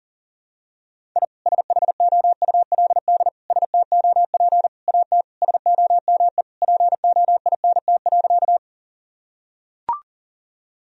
Morse code, heard as a keyed tone, about 37 wpm, 700 Hz. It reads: I SHOULD STOP AT SOME POINT.
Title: Sentences from Top 1000 Words - Review Part 1 - 40wpm